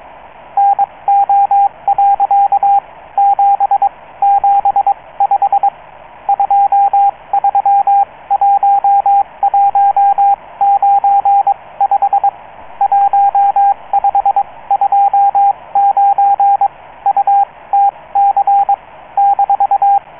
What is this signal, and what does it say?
Morse code, heard as a keyed tone, about 22 wpm, spaced as for 18 wpm, 800 Hz, with radio noise. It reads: NO. 775 231195 1529 UTC -